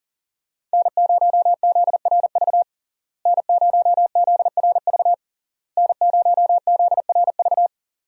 Morse code, heard as a keyed tone, about 40 wpm, 700 Hz. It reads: N0ZRV N0ZRV N0ZRV
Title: USA Call Signs - 5 Characters - Repeated 3x - 40wpm